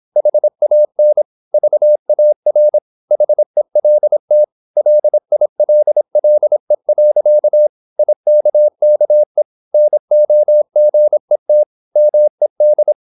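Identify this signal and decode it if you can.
Morse code, heard as a keyed tone, about 26 wpm, 600 Hz. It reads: HAN VAR HELT LILLE. IKKE NOGET MED
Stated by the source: High-speed (QRQ) Morse at 26 wpm.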